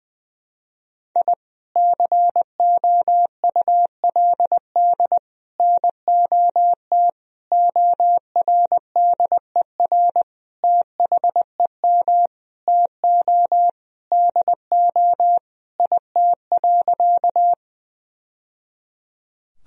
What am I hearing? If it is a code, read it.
Morse code, heard as a keyed tone, about 20 wpm, 700 Hz. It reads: I COULD NOT ORDER THEM TO DO IT.
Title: Sentences from Top 1000 Words - Review Part 2 - 20wpm